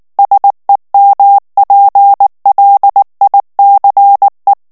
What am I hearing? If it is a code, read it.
Morse code, heard as a keyed tone, about 19 wpm, 800 Hz. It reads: SEMPLICE
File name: morse mp3 / ITALIEN texte clair 2 vitesse 18 mots minute